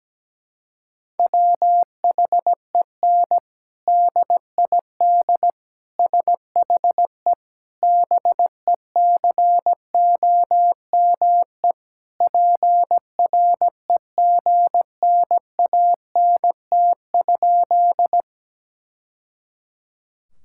Morse code, heard as a keyed tone, about 17 wpm, 700 Hz. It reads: WHEN DID SHE BECOME PREGNANT?
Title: Sentences from Top 1000 Words - Review Part 6 - 17wpm